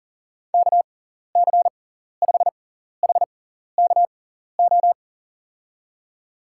Morse code, heard as a keyed tone, about 39 wpm, spaced as for 14 wpm, 700 Hz. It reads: KC5HKO
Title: USA Call Signs - 6 Characters - 6x Character Spacing - 40wpm